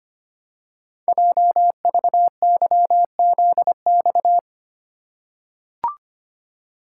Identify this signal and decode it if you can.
Morse code, heard as a keyed tone, about 25 wpm, 700 Hz. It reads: JVYZX
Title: Code Groups - 5 Letters - 25wpm